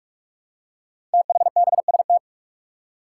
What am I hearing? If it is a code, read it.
Morse code, heard as a keyed tone, about 44 wpm, 700 Hz. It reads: THBST
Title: Code Groups - 5 Letters - 45wpm